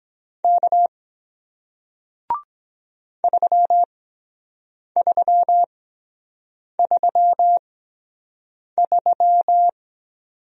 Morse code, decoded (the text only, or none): K 3333